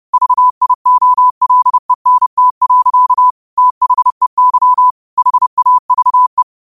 Morse code, heard as a keyed tone, about 30 wpm, 1000 Hz. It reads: VIOLENT. THEY HAVE